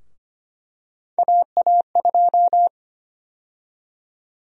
Morse code, heard as a keyed tone, about 25 wpm, 700 Hz. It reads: AA2
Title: USA Call Sign Prefixes - 25wpm